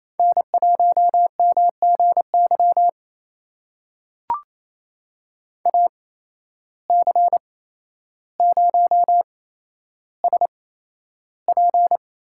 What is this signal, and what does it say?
Morse code, heard as a keyed tone, about 28 wpm, spaced as for 12 wpm, 700 Hz. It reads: N1MGY AC0SP
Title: USA Call Signs - 8x Character Spacing - 28wpm